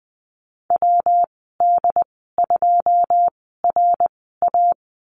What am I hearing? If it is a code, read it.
Morse code, heard as a keyed tone, about 20 wpm, spaced as for 14 wpm, 700 Hz. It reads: WD2RA
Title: 161 Lesson 26g - Letter g - Call Signs 20wpm